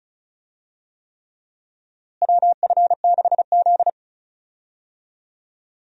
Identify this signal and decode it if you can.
Morse code, heard as a keyed tone, about 34 wpm, 700 Hz. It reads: WF6Z
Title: USA Call Signs - 4 Characters - Repeated 3x - 35wpm